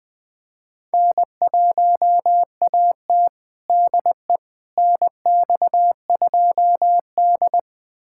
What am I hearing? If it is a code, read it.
Morse code, heard as a keyed tone, about 21 wpm, 700 Hz. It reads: N1AT DE NX2D
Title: USA Callsign DE Callsign - 4 Characters - 20wpm